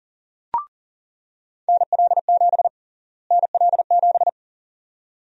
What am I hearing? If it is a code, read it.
Morse code, heard as a keyed tone, about 40 wpm, 700 Hz. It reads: NL7 NL7
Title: USA Call Sign Prefixes - Repeated 2x - 40wpm